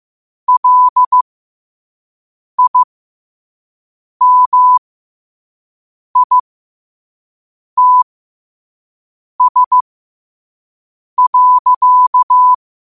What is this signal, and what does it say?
Morse code, heard as a keyed tone, about 15 wpm, spaced as for 5 wpm, 1000 Hz. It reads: LIMITS.
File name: qotdfiles / 5wpm